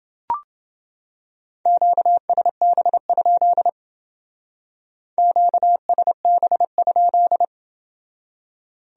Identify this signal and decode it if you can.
Morse code, heard as a keyed tone, about 28 wpm, 700 Hz. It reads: QSB? QSB?